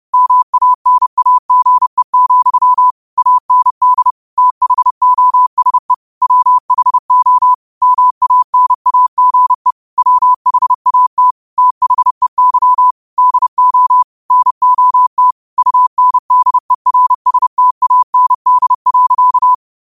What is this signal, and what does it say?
Morse code, heard as a keyed tone, about 30 wpm, 1000 Hz. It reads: MANAGE, AND THOSE WHO MANAGE WHAT THEY DO NOT UNDERSTAND.